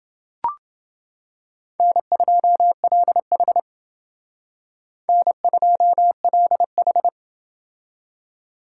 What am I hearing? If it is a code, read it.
Morse code, heard as a keyed tone, about 29 wpm, 700 Hz. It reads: N2LH N2LH